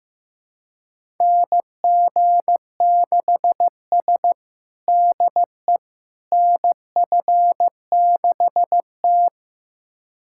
Morse code, heard as a keyed tone, about 15 wpm, 700 Hz. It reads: NG6S DE NF6T
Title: USA Callsign DE Callsign - 4 Characters - 15wpm